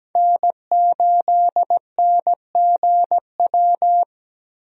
Morse code, heard as a keyed tone, about 17 wpm, 700 Hz. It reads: N8NGW
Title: USA Call Signs - 5 Characters - Repeated 2x - 17wpm